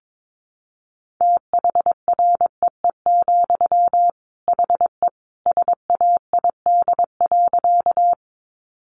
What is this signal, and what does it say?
Morse code, heard as a keyed tone, about 22 wpm, 700 Hz. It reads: THREE, HE SAID.
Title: Sentences from Top 200 Words 22wpm